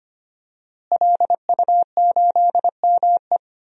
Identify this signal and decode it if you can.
Morse code, heard as a keyed tone, about 25 wpm, 700 Hz. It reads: LU8ME
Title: DXCC-100 Call Signs - 5 Characters - Repeated 2x - 25wpm